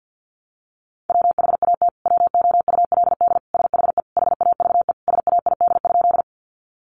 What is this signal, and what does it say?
Morse code, heard as a keyed tone, about 50 wpm, 700 Hz. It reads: WHAT WOULD SHE HAVE SAID?
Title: Sentences from Top 100 Words 50wpm